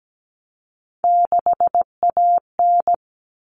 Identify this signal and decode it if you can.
Morse code, heard as a keyed tone, about 17 wpm, 700 Hz. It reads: 6AN
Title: All Cut Numbers - 3-Digit 17wpm